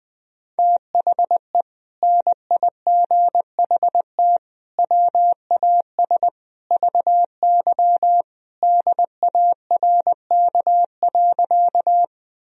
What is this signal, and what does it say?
Morse code, heard as a keyed tone, about 20 wpm, 700 Hz. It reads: THE NIGHT WAS VY DARK.
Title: Sentences from Top 500 Words and Abbreviations 20wpm